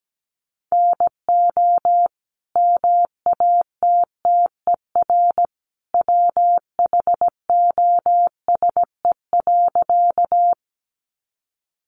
Morse code, heard as a keyed tone, about 17 wpm, 700 Hz. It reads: NO MATTER WHOSE.
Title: Sentences from Top 1200 Words 17wpm